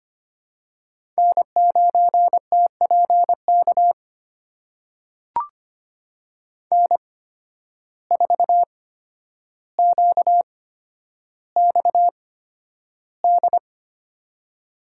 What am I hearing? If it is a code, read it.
Morse code, heard as a keyed tone, about 25 wpm, spaced as for 11 wpm, 700 Hz. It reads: N9TPK N4QXD
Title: USA Call Signs - 5 Characters - 8x Character Spacing - 25wpm